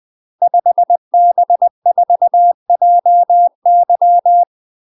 Morse code, heard as a keyed tone, about 20 wpm, 700 Hz. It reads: 5B4JY
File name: UK MASTER QSOs TRAINING / 20WPM / UK MASTER QSOs 0037